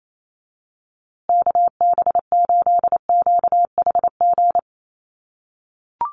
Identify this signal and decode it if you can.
Morse code, heard as a keyed tone, about 28 wpm, 700 Hz. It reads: KB8QHG
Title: USA Call Signs Repeated 2x 28wpm